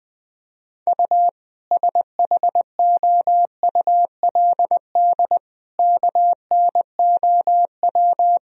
Morse code, heard as a keyed tone, about 20 wpm, 700 Hz. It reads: U SHOULD KNOW
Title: Sentences from Top 400 Words and Abbreviations 20wpm